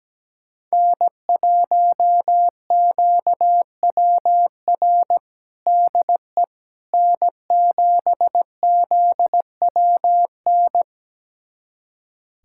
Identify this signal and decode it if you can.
Morse code, heard as a keyed tone, about 17 wpm, 700 Hz. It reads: N1QWR DE N7ZWN
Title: USA Callsign DE Callsign - 5 Characters - 17wpm